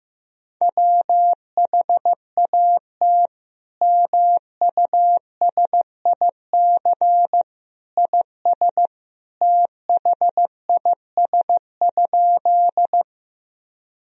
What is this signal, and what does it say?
Morse code, heard as a keyed tone, about 15 wpm, 700 Hz. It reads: WHAT MUSIC IS THIS?
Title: Sentences from Top 1000 Words - Review Part 1 - 15wpm